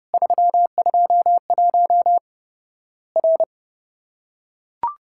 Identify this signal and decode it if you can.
Morse code, heard as a keyed tone, about 32 wpm, 700 Hz. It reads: 321 R
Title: Worked All Europe DX Contest - QTC Exchange - 30wpm